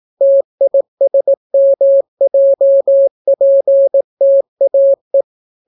Morse code, heard as a keyed tone, about 18 wpm, 550 Hz. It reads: TISMJPTAE